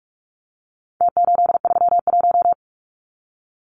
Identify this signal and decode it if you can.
Morse code, heard as a keyed tone, about 44 wpm, 700 Hz. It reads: T831
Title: Numbers - Contest Style - 4-Digit 45wpm